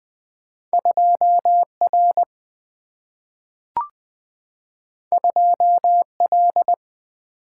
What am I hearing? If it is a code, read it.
Morse code, heard as a keyed tone, about 20 wpm, 700 Hz. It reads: 2 R 2 L
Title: Single-Letter Single-Number 20wpm